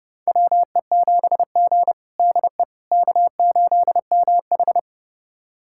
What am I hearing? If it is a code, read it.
Morse code, heard as a keyed tone, about 31 wpm, 700 Hz. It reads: WE7G DE K8MH
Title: USA Callsign DE Callsign - 4 Characters - 30wpm